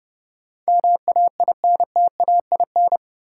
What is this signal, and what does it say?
Morse code, heard as a keyed tone, about 30 wpm, 700 Hz. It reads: MAINTAIN
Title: Sentences from Top 1000 Words - Review Part 5 - New Word Drill - 30wpm